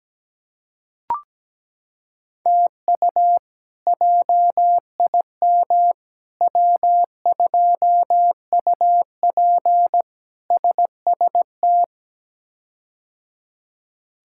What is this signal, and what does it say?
Morse code, heard as a keyed tone, about 17 wpm, 700 Hz. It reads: TU JIM W2UP SST
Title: K1USN SST QSO Practice - 17wpm